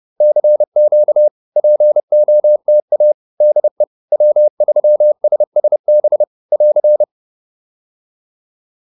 Morse code, heard as a keyed tone, about 30 wpm, 600 Hz. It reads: CQ POTA DE W3SSB <AR>